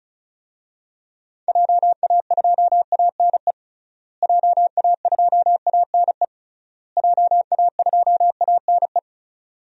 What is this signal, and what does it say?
Morse code, heard as a keyed tone, about 36 wpm, 700 Hz. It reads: JA2ANE JA2ANE JA2ANE
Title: DXCC-100 Call Signs - 6 Characters - Repeated 3x - 35wpm